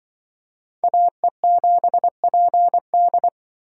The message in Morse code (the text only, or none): AE7PD